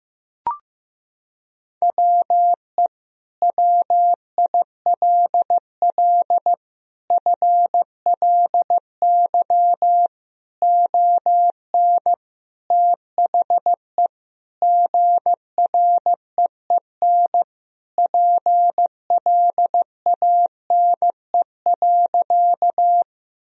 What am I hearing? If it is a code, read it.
Morse code, heard as a keyed tone, about 15 wpm, 700 Hz. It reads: WE WILL FLY ON THE GREEN PLANE.
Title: Sentences from Top 1000 Words - Review Part 3 - 15wpm